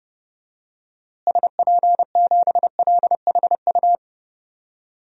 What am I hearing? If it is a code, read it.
Morse code, heard as a keyed tone, about 30 wpm, 700 Hz. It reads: SP7LHU
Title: DXCC-100 Call Signs - 6 Characters - Repeated 3x - 30wpm